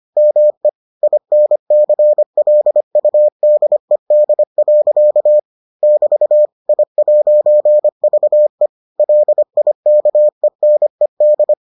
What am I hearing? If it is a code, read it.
Morse code, heard as a keyed tone, about 25 wpm, 600 Hz. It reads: ME INCLUDED. <BT> I'VE LIKENED